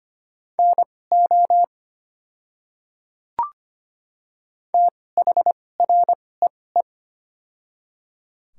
Morse code, heard as a keyed tone, about 25 wpm, spaced as for 18 wpm, 700 Hz. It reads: NO THREE
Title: 2x Character Spacing - Top 200 Words - 25wpm